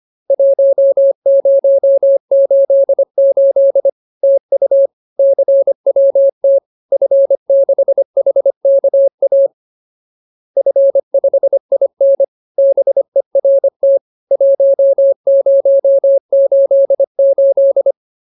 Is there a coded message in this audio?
Yes: Morse code from 0.3 to 17.9 s, heard as a keyed tone, about 25 wpm, 550 Hz. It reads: 1088 TU CWT F6HKA F5IN BERT 1088